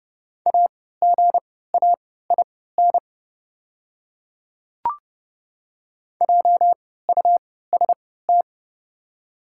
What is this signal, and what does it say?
Morse code, heard as a keyed tone, about 30 wpm, spaced as for 17 wpm, 700 Hz. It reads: AGAIN JUST
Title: 3x Character Spacing - Top 200 Words - 30wpm